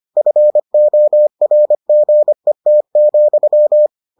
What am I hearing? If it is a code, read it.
Morse code, heard as a keyed tone, about 25 wpm, 600 Hz. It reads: FORGET,